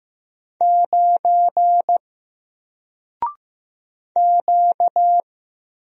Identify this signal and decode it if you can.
Morse code, heard as a keyed tone, about 15 wpm, 700 Hz. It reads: 9 Q
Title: Single Character (Letter:Number:Punctuation:Prosign) 15wpm